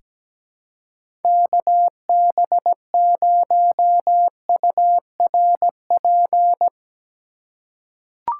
Morse code, heard as a keyed tone, about 17 wpm, 700 Hz. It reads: KB0URP
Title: USA Call Signs - 6 Characters - 4x Character Spacing - 17wpm